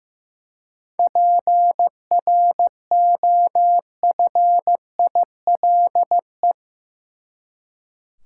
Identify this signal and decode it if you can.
Morse code, heard as a keyed tone, about 15 wpm, 700 Hz. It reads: PROFILE
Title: Seven Letter Words - 15wpm